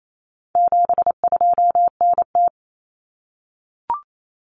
Morse code, heard as a keyed tone, about 28 wpm, 700 Hz. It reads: 72NT
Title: Common Cut Numbers - 4-Digit 28wpm